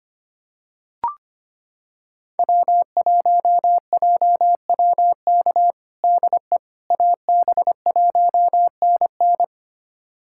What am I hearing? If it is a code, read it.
Morse code, heard as a keyed tone, about 25 wpm, 700 Hz. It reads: W1JWK DE AB1NN
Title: USA Callsign DE Callsign - 5 Characters - 25wpm